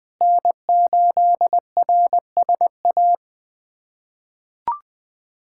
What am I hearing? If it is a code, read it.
Morse code, heard as a keyed tone, about 20 wpm, 700 Hz. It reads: N8RSA
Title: USA Callsign DE Callsign - 5 Characters - 20wpm